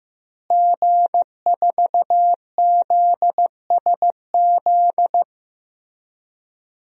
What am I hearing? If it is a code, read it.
Morse code, heard as a keyed tone, about 15 wpm, 700 Hz. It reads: G4ZSZ
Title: DXCC-100 Call Signs - 5 Characters - Repeated 3x - 15wpm